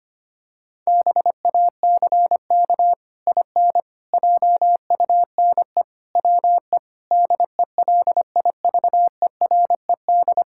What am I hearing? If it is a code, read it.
Morse code, heard as a keyed tone, about 25 wpm, 700 Hz. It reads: BACK IN JUNE WE DELIVERED
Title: Pangram Challenge v2 - 25wpm